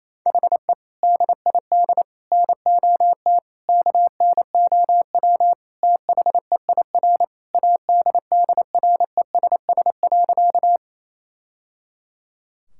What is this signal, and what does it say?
Morse code, heard as a keyed tone, about 28 wpm, 700 Hz. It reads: HE DID NOT KNOW THEIR ADDRESS.
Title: Sentences from Top 1000 Words - Review Part 6 - 28wpm